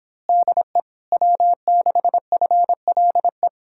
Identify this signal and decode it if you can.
Morse code, heard as a keyed tone, about 26 wpm, 700 Hz. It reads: DE W6FLE
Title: Speed-Racing Common QSO Phrases - 22wpm